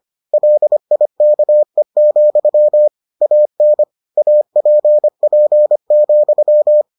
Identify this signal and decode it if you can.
Morse code, heard as a keyed tone, about 25 wpm, 600 Hz. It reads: LIKE, AN APP,